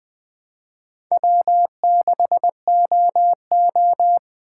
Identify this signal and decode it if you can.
Morse code, heard as a keyed tone, about 20 wpm, 700 Hz. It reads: W6OO
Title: USA Call Signs - 4 Characters - 6x Character Spacing - 20wpm